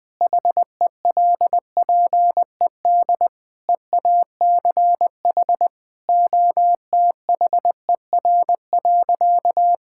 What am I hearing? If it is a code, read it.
Morse code, heard as a keyed tone, about 20 wpm, 700 Hz. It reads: HELPED EACH OTHER.
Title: Sentences with 5-Word Phrases - 20wpm